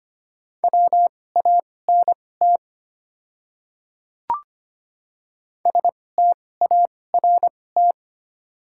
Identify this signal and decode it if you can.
Morse code, heard as a keyed tone, about 25 wpm, spaced as for 18 wpm, 690 Hz. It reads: WANT START
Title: 2x Character Spacing - Top 300 Words - 25wpm